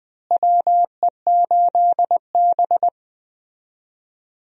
Morse code, heard as a keyed tone, about 20 wpm, 700 Hz. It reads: WE8B
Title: USA Call Signs - 4 Characters - Repeated 2x - 20wpm